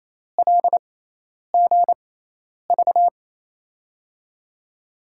Morse code, heard as a keyed tone, about 28 wpm, spaced as for 10 wpm, 700 Hz. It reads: LGV
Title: USA Call Signs - 6x Character Spacing - 28wpm